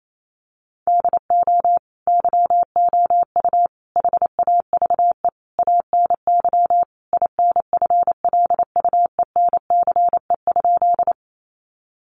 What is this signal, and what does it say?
Morse code, heard as a keyed tone, about 28 wpm, 700 Hz. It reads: DO YOU HAVE ANY INFLUENCE?
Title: Sentences from Top 1100 Words 28wpm